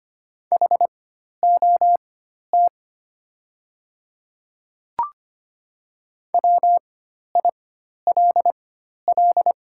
Morse code, heard as a keyed tone, about 25 wpm, spaced as for 12 wpm, 700 Hz. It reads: HOT WILL
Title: 4x Character Spacing - Top 100 Words - 25wpm